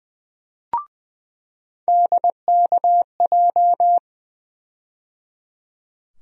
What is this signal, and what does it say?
Morse code, heard as a keyed tone, about 20 wpm, 700 Hz. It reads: DKJ
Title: USA Call Sign Postfixes - 3 Letters 20wpm